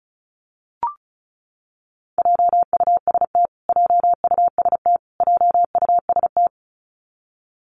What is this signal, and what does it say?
Morse code, heard as a keyed tone, about 34 wpm, 700 Hz. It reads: JUST JUST JUST
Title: Top 500 Words Repeated 35wpm